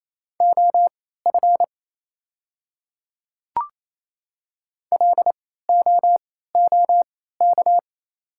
Morse code, heard as a keyed tone, about 27 wpm, spaced as for 16 wpm, 690 Hz. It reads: OF LOOK
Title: 3x Character Spacing - Top 100 Words - 28wpm